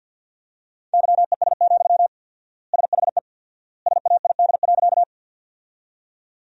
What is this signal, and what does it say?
Morse code, heard as a keyed tone, about 53 wpm, 700 Hz. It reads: YES, SHE SAID.
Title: 4x Word-Spacing - Sentences from Top 500 Words - 50wpm